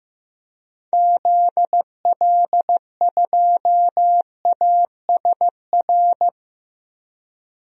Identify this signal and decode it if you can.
Morse code, heard as a keyed tone, about 15 wpm, 700 Hz. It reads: ZL2ASR
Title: DXCC-100 Call Signs - 6 Characters - 15wpm